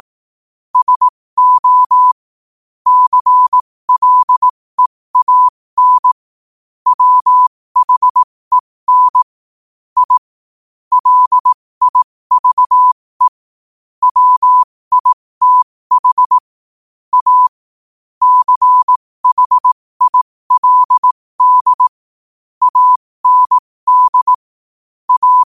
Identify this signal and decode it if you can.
Morse code, heard as a keyed tone, about 18 wpm, spaced as for 15 wpm, 1000 Hz. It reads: SO CLEAN WHEN I LIVE WITH A CHILD AND A
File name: qotdfiles / 15wpm